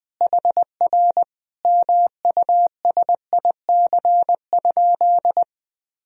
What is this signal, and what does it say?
Morse code, heard as a keyed tone, about 21 wpm, 700 Hz. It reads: HR MUSIC?
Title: Sentences from Top 300 Words and Abbreviations 20wpm